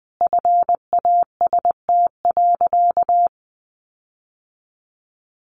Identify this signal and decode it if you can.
Morse code, heard as a keyed tone, about 20 wpm, 700 Hz. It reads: FAST.
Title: Sentences from Top 1300 Words 20wpm